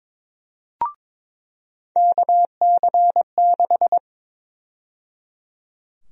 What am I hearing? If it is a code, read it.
Morse code, heard as a keyed tone, about 22 wpm, 700 Hz. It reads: KC6
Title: USA Call Sign Prefixes - 22wpm